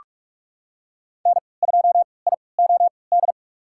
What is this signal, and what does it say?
Morse code, heard as a keyed tone, about 42 wpm, spaced as for 25 wpm, 700 Hz. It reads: N2IOD
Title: USA Call Signs - 5 Characters - 3x Character Spacing - 45wpm